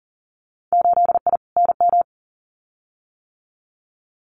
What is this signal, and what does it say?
Morse code, heard as a keyed tone, about 40 wpm, 700 Hz. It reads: 8I NM
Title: Winter Field Day Exchanges Encore 40wpm